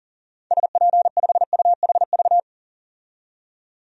Morse code, heard as a keyed tone, about 40 wpm, 700 Hz. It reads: SP5UHV
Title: DXCC-100 Call Signs - 6 Characters - 40wpm